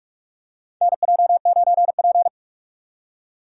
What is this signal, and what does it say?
Morse code, heard as a keyed tone, about 44 wpm, 700 Hz. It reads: NJ9P
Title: USA Call Signs - 3x Character Spacing - 45wpm